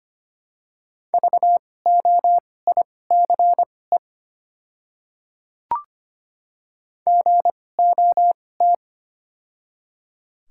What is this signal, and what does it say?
Morse code, heard as a keyed tone, about 25 wpm, spaced as for 18 wpm, 700 Hz. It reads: VOICE GOT
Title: 2x Character Spacing - Top 500 Words - 25wpm